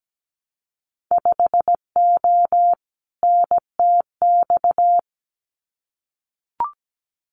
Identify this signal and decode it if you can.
Morse code, heard as a keyed tone, about 17 wpm, 700 Hz. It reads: 5O NTX
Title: Winter Field Day Exchanges Encore 17wpm